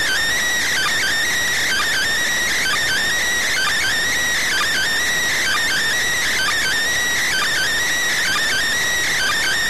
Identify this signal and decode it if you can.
Emergency Alert System (SAME) tones (header bursts), heard as harsh buzzing bursts. It is heard from 1.0 to 8.8 s.